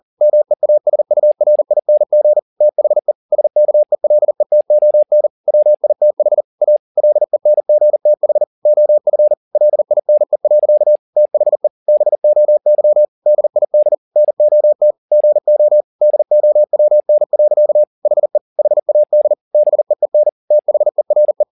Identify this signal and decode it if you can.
Morse code, heard as a keyed tone, about 40 wpm, 600 Hz. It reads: MEASURING THE SKELETON WITH A LENGTH OF LINE. THE BOY DID NOT GO DOWN. HE HAD BEEN THERE
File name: the old man and the sea 2 40wpm